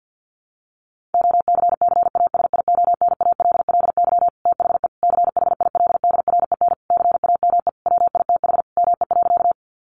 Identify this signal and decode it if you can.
Morse code, heard as a keyed tone, about 50 wpm, 700 Hz. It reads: OCCASIONALLY THE CHILDREN CAME WITH ME.
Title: Sentences from Top 1300 Words 50wpm